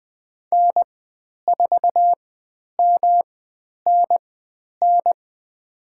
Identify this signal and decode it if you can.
Morse code, heard as a keyed tone, about 20 wpm, spaced as for 10 wpm, 700 Hz. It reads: N4MNN
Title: 109 Lesson 18g - Letter m - Call Signs 20wpm